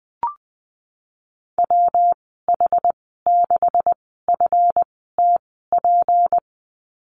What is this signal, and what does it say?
Morse code, heard as a keyed tone, about 19 wpm, spaced as for 14 wpm, 700 Hz. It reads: WH6FTP
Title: 173 Lesson 28g - Number 9 - Call Signs 20wpm